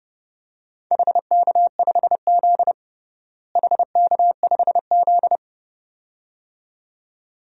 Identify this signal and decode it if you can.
Morse code, heard as a keyed tone, about 32 wpm, 700 Hz. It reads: HK5Z HK5Z
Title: DXCC-100 Call Signs - 4 Characters - Repeated 2x - 30wpm